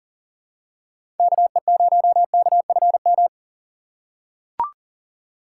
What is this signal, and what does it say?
Morse code, heard as a keyed tone, about 40 wpm, 700 Hz. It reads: KE0KFM
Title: USA Call Signs - 4x Character Spacing - 40wpm